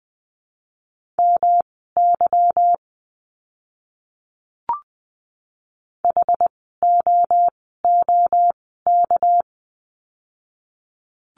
Morse code, heard as a keyed tone, about 20 wpm, spaced as for 14 wpm, 700 Hz. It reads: MY HOOK
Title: 199 Lesson 32f - Letter k - Words 20wpm